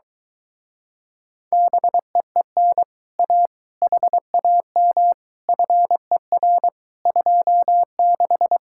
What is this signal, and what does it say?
Morse code, heard as a keyed tone, about 23 wpm, 700 Hz. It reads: BEEN A HAM FER 26